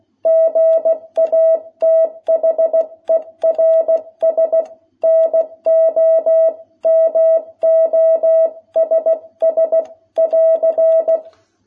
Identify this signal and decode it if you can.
Morse code, heard as a keyed tone, about 16 wpm, 660 Hz. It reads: GATHERS NO MOSS <AR>
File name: jh8cbh / 2021cw